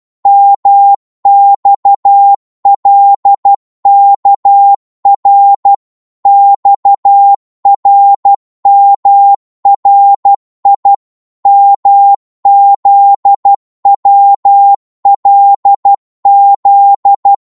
Morse code, heard as a keyed tone, about 12 wpm, 800 Hz. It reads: MXLKR XRMRI MZWLZ
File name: Morse / ltr-12wpm